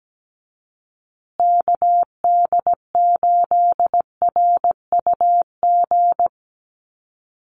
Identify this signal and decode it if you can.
Morse code, heard as a keyed tone, about 17 wpm, 700 Hz. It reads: KD8RUG
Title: USA Call Signs Repeated 2x 17wpm